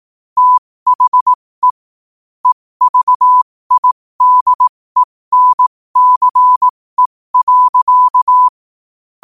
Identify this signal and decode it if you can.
Morse code, heard as a keyed tone, about 18 wpm, spaced as for 15 wpm, 1000 Hz. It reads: THE EVIDENCE.
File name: qotdfiles / 15wpm